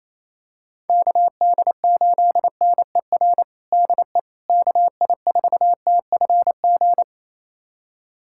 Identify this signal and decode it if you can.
Morse code, heard as a keyed tone, about 28 wpm, 700 Hz. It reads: KD8NER DE KI4TFG
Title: USA Callsign DE Callsign - 6 Characters - 28wpm